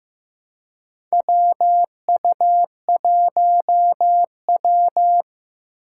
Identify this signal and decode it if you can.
Morse code, heard as a keyed tone, about 15 wpm, 700 Hz. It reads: WU1W
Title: USA Call Signs - 4 Characters - 3x Character Spacing - 15wpm